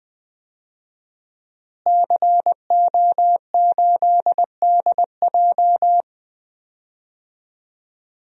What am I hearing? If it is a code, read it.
Morse code, heard as a keyed tone, about 20 wpm, 700 Hz. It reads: CO8DJ
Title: DXCC-100 Call Signs - 5 Characters - Repeated 3x - 20wpm